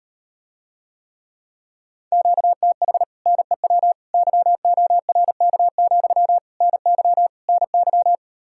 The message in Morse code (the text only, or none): QTH NEW YORK, NY NY